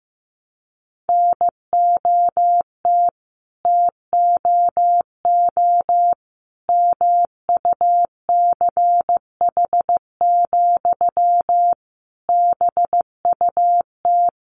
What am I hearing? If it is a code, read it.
Morse code, heard as a keyed tone, about 15 wpm, 700 Hz. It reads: NOT TOO MUCH, BUT
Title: Sentences from Top 1000 Words 15wpm